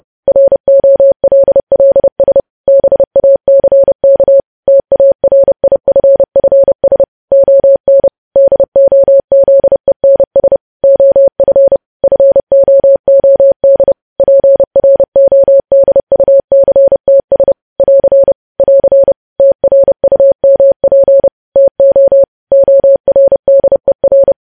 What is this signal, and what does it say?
Morse code, heard as a keyed tone, about 30 wpm, 550 Hz. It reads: ROLLS BACK TARIFFS ON DOZENS OF FOOD PRODUCTS <AR> <AR> TRUMP TO ORDER
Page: News Headlines in Morse Code at 30 WPM